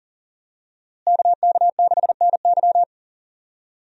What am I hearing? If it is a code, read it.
Morse code, heard as a keyed tone, about 40 wpm, 700 Hz. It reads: KK6NY
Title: USA Call Signs - 5 Characters - Repeated 2x - 40wpm